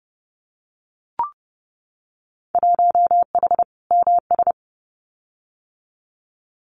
Morse code, heard as a keyed tone, about 30 wpm, 700 Hz. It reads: 1H MS
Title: Winter Field Day Exchanges 30wpm